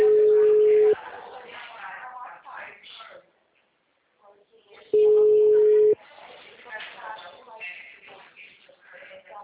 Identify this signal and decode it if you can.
Ringback tone (Europe, 425 Hz), with speech.
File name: out- -2001- - - .27343